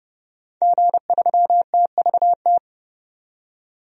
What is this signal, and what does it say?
Morse code, heard as a keyed tone, about 30 wpm, 700 Hz. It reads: G3TVT
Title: DXCC-100 Call Signs - 5 Characters - 30wpm